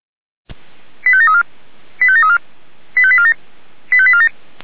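Five-tone selective call (ZVEI), heard as quick runs of five tones, tones of about 70 ms. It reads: 85734 85734 85847 85847
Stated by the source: ZVEI five-tone selective call 85385 received by SDR.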